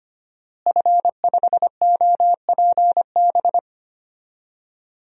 Morse code, heard as a keyed tone, about 25 wpm, 700 Hz. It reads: F5OPB
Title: DXCC-100 Call Signs - 5 Characters - Repeated 2x - 25wpm